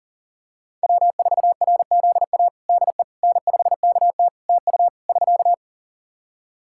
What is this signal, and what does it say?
Morse code, heard as a keyed tone, about 37 wpm, 700 Hz. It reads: W4RZA DE N5KT TU <SK>
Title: Practice QSO - Exchange 4 - Ending - 40wpm